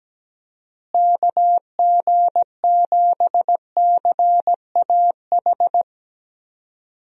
Morse code, heard as a keyed tone, about 17 wpm, 700 Hz. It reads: KG7CAH
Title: USA Call Signs - 6 Characters - Repeated 2x - 17wpm